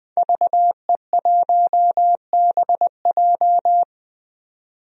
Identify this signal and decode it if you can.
Morse code, heard as a keyed tone, about 20 wpm, 700 Hz. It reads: VE1BJ
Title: DXCC-100 Call Signs - 5 Characters - 20wpm